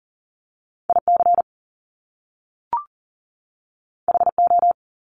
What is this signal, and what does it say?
Morse code, heard as a keyed tone, about 42 wpm, 700 Hz. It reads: IC HO
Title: Top Two Letter Combinations in Words 40wpm